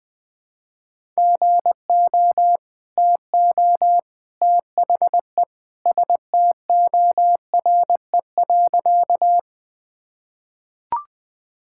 Morse code, heard as a keyed tone, about 20 wpm, 700 Hz. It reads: GO TO THE STORE.
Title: Sentences with 2-Word Phrases - 20wpm